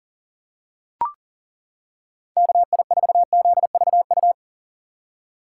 Morse code, heard as a keyed tone, about 40 wpm, 700 Hz. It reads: KI4ZVU
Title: K1USN SST QSO Practice - 40wpm